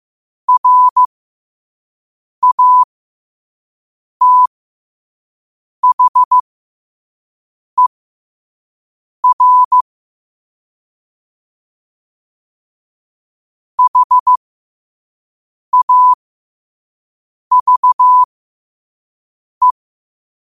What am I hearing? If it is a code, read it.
Morse code, heard as a keyed tone, about 15 wpm, spaced as for 5 wpm, 1000 Hz. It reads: RATHER HAVE
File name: qotdfiles / 5wpm